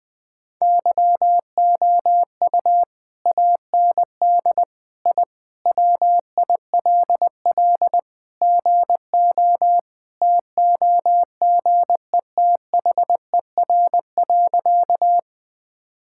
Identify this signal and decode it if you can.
Morse code, heard as a keyed tone, about 21 wpm, 700 Hz. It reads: YOU AND I WILL GO TOGETHER.
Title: Sentences with 2-Word Phrases - 20wpm